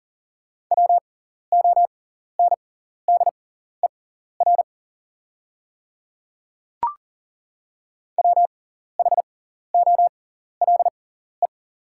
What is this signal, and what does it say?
Morse code, heard as a keyed tone, about 40 wpm, spaced as for 14 wpm, 690 Hz. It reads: WONDER WHOLE
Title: 6x Character Spacing - Top 500 Words - 40wpm